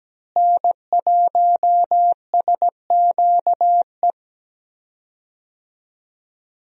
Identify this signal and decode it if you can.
Morse code, heard as a keyed tone, about 17 wpm, 700 Hz. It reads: N1SQE
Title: USA Call Signs - 5 Characters - Repeated 2x - 17wpm